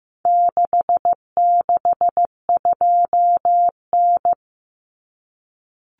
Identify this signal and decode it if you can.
Morse code, heard as a keyed tone, about 15 wpm, 700 Hz. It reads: 662N
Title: Common Cut Numbers - 4-Digit 15wpm